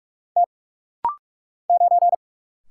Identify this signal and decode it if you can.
Morse code, heard as a keyed tone, about 45 wpm, 700 Hz. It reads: T 9
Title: Single Letter-Number - Rapid-Fire 45wpm